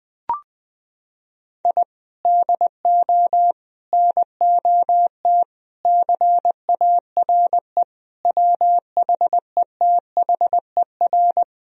Morse code, heard as a keyed tone, about 20 wpm, 700 Hz. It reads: I DO NOT CARE WHETHER
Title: Sentences from Top 1000 Words - Review Part 3 - 20wpm